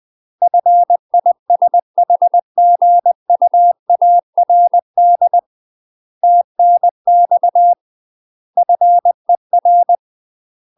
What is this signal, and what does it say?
Morse code, heard as a keyed tone, about 20 wpm, 700 Hz. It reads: FISHGUARD TNX FER
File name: UK MASTER QSOs TRAINING / 20WPM / UK MASTER QSOs 0012